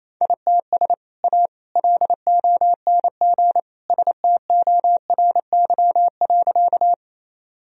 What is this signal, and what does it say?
Morse code, heard as a keyed tone, about 28 wpm, 700 Hz. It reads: ITS A LONG STORY.
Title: Sentences from Top 1000 Words - Review Part 1 - 28wpm